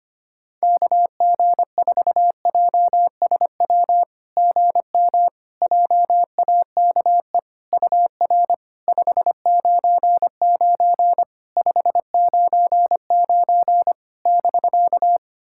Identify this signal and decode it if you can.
Morse code, heard as a keyed tone, about 25 wpm, 690 Hz. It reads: KG4JSW GM JAKE UR 599 599 <BK>
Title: SOTA QSO Practice 25wpm